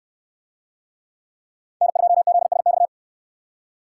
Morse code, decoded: N2ZSX

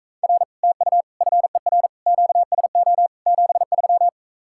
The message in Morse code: R TU FER QSO 73

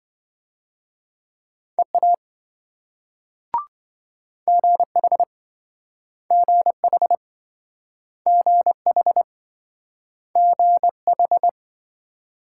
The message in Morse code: EA GH GH GH GH

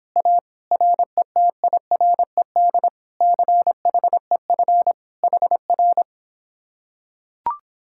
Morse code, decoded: A RETIRED CHEF HR

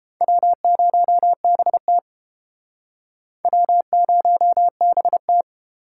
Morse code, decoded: W0BT W0BT

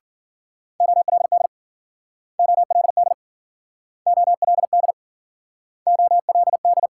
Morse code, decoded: OLD OLD OLD OLD

